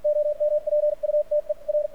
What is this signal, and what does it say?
Morse code, heard as a keyed tone, about 56 wpm, spaced as for 45 wpm, 600 Hz, with radio noise. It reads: OMPUTER
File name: archived / CRSnet